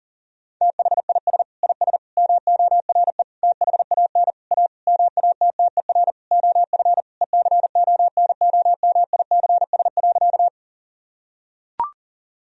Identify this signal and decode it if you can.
Morse code, heard as a keyed tone, about 40 wpm, 700 Hz. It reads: THIS IS MORE THAN A MATTER OF ECONOMICS.
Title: Sentences from Top 1000 Words - Review Part 3 - 40wpm